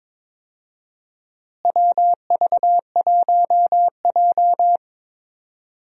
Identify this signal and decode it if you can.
Morse code, heard as a keyed tone, about 21 wpm, 700 Hz. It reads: WV1J
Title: USA Call Signs - 4 Characters - Repeated 2x - 22wpm